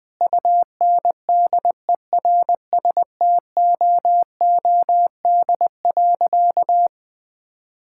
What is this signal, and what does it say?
Morse code, heard as a keyed tone, about 20 wpm, 700 Hz. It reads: UNDERSTOOD.
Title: Sentences with 5-Word Phrases - 20wpm